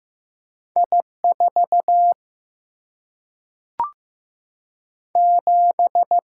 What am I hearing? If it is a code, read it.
Morse code, heard as a keyed tone, about 15 wpm, 700 Hz. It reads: I 4 7
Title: Single-Letter Single-Number 15wpm